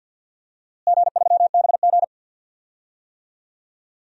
Morse code, heard as a keyed tone, about 52 wpm, 700 Hz. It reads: G3BG
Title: DXCC-100 Call Signs - 4 Characters - Repeated 2x - 50wpm